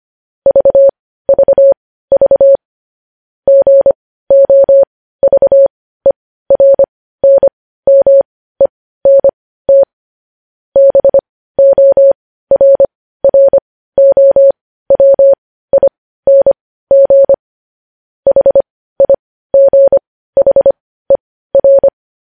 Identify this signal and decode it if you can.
Morse code, heard as a keyed tone, about 25 wpm, spaced as for 15 wpm, 550 Hz. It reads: VVV GOVERNMENT BORROWING HIGHER
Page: news headlines in morse code at 15 wpm